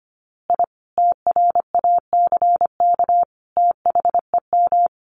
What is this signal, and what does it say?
Morse code, heard as a keyed tone, about 25 wpm, 700 Hz. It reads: I TRACK THEM
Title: A Sound Of Thunder 25wpm